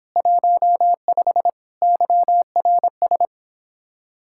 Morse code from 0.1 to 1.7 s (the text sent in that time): 15